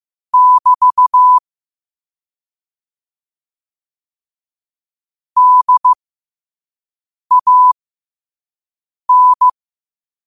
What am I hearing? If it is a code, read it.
Morse code, heard as a keyed tone, about 15 wpm, spaced as for 5 wpm, 1000 Hz. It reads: <BT> DAN
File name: qotdfiles / 5wpm